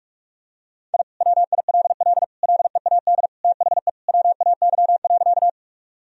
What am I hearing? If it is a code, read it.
Morse code, heard as a keyed tone, about 46 wpm, 700 Hz. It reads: I WILL LEAD THE WAY.
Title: Sentences from Top 1000 Words - Review Part 2 - 45wpm